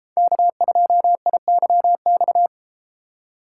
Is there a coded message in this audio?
Yes: Morse code from 0.2 to 2.5 s, heard as a keyed tone, about 30 wpm, 700 Hz. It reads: K2IYX